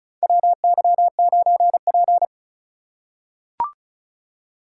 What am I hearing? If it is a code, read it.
Morse code, heard as a keyed tone, about 36 wpm, 700 Hz. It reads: WY9P
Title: USA Call Signs - 4 Characters - 4x Character Spacing - 35wpm